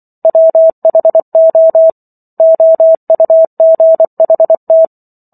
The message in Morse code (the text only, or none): WHO OUGHT